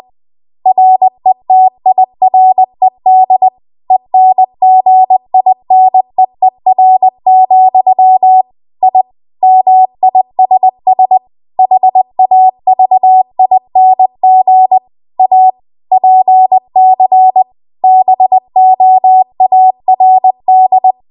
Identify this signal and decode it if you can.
Morse code, heard as a keyed tone, about 20 wpm, 750 Hz. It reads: RETIRED ENGINEER, I MISS HAVING A PC BOARD
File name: Archive / 20 WPM / 20WPM